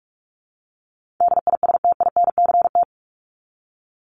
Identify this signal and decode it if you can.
Morse code, heard as a keyed tone, about 44 wpm, 700 Hz. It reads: DISTINCT
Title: Top 1501-1750 Words 45wpm